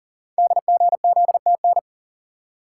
Morse code, heard as a keyed tone, about 42 wpm, 690 Hz. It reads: DGZTN
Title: Code Groups - 5 Letter:Numbers - 40wpm